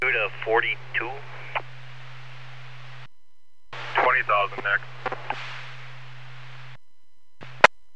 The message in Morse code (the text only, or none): none